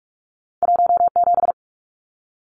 Morse code, heard as a keyed tone, about 46 wpm, 700 Hz. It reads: 17
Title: Numbers - 2-Digit 45wpm